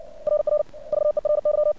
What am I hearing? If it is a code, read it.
Morse code, heard as a keyed tone, about 56 wpm, 600 Hz. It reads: RR VERY
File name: archived / CRSnet